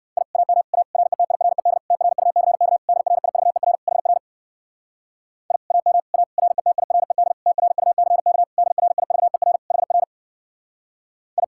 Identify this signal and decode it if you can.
Morse code, heard as a keyed tone, about 69 wpm, 700 Hz. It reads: I AM A RETIRED TRUCK DRIVER HR I AM A RETIRED TRUCK DRIVER HR I